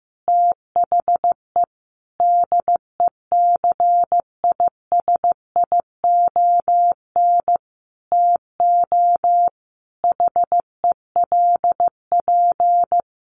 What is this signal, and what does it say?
Morse code, heard as a keyed tone, about 15 wpm, 700 Hz. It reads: THE DECISION TO HELP
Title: Sentences from Top 1400 Words 15wpm